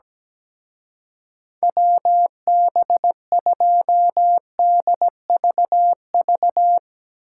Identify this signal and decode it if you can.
Morse code, heard as a keyed tone, about 17 wpm, 700 Hz. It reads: WB2DVV
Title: Speed Racing - USA 2x3 Call Signs 17wpm